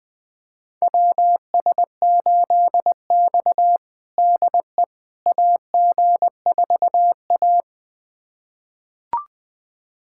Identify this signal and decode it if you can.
Morse code, heard as a keyed tone, about 20 wpm, 700 Hz. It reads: WS8X DE AG4A
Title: USA Callsign DE Callsign - 4 Characters - 20wpm